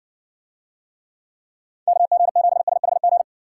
Morse code, heard as a keyed tone, about 60 wpm, 700 Hz. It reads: KG7SHG